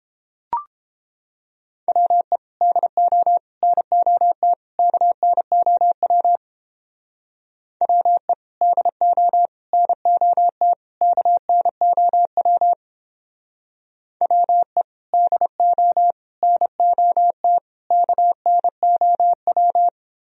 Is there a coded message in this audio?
Yes: Morse code from 1.9 to 19.9 s, heard as a keyed tone, about 29 wpm, 700 Hz. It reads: WE DO NOT KNOW WE DO NOT KNOW WE DO NOT KNOW